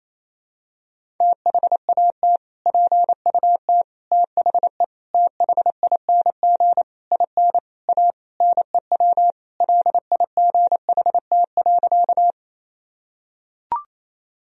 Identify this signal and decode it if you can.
Morse code, heard as a keyed tone, about 28 wpm, 700 Hz. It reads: THAT PUT THE THING IN A NEW LIGHT.
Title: Sentences from Top 1000 Words - Review Part 1 - 28wpm